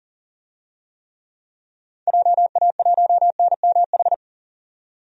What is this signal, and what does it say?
Morse code, heard as a keyed tone, about 40 wpm, 700 Hz. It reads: JA1NMH
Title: DXCC-100 Call Signs - 6 Characters - 40wpm